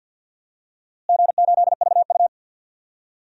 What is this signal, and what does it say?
Morse code, heard as a keyed tone, about 48 wpm, 700 Hz. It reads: G8VU
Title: DXCC-100 Call Signs - 4 Characters - Repeated 3x - 50wpm